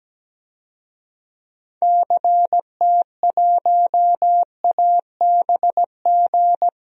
Morse code, heard as a keyed tone, about 17 wpm, 700 Hz. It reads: CT1ABG
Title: DXCC-100 Call Signs - 6 Characters - Repeated 2x - 17wpm